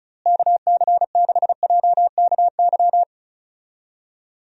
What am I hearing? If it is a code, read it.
Morse code, heard as a keyed tone, about 36 wpm, 700 Hz. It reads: KC6JKY
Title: USA Call Signs - 6 Characters - Repeated 2x - 35wpm